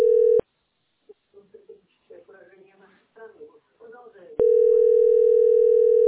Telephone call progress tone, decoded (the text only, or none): ringback tone